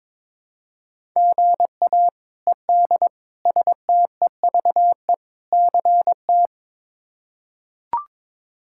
Morse code, heard as a keyed tone, about 22 wpm, 700 Hz. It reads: GA ED STEVE CT
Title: K1USN SST QSO Practice - 22wpm